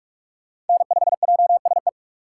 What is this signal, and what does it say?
Morse code, heard as a keyed tone, about 44 wpm, 700 Hz. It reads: N5JSE